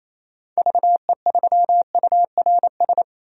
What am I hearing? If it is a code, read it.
Morse code, heard as a keyed tone, about 28 wpm, 700 Hz. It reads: VE3URS
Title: DXCC-100 Call Signs - 6 Characters - 28wpm